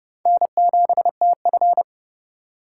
Morse code, heard as a keyed tone, about 36 wpm, 700 Hz. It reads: N7TF
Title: Speed Racing - USA 1x2 Call Signs 30wpm